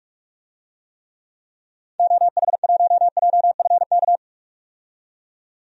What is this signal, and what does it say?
Morse code, heard as a keyed tone, about 46 wpm, 700 Hz. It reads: OH1JFK
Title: DXCC-100 Call Signs - 6 Characters - 45wpm